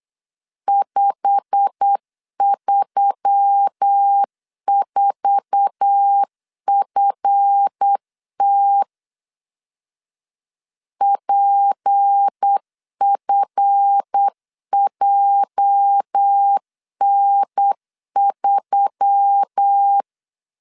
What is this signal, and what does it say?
Morse code, heard as a keyed tone, about 8 wpm, 800 Hz. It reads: 534FT PFJN3
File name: LEZIONI / 9 2 8wpm 3437